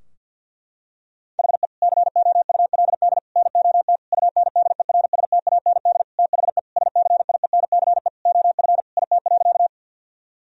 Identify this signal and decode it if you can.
Morse code, heard as a keyed tone, about 50 wpm, 700 Hz. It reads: HE COULD NOT UNDERSTAND THE SCIENCE OF IT.
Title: Sentences from Top 1000 Words - Review Part 2 - 50wpm